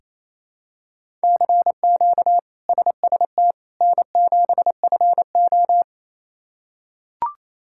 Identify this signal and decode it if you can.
Morse code, heard as a keyed tone, about 27 wpm, 700 Hz. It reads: CQ SST N7FO
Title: K1USN SST QSO Practice - 28wpm